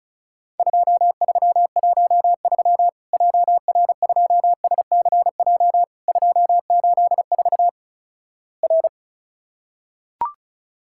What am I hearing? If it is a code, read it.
Morse code, heard as a keyed tone, about 36 wpm, 700 Hz. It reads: 2313 JR2SCJ 284 R
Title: Worked All Europe DX Contest - QTC Exchange - 35wpm